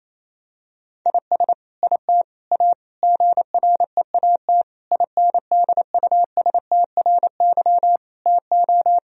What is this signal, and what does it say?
Morse code, heard as a keyed tone, about 27 wpm, 700 Hz. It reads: IS IT A GREAT INDUSTRY TO
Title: Sentences from Top 1000 Words - Review Part 4 - 28wpm